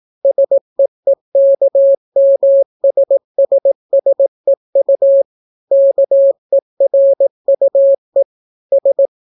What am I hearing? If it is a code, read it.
Morse code, heard as a keyed tone, about 17 wpm, 550 Hz. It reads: SEEKMSSSEU KERUE S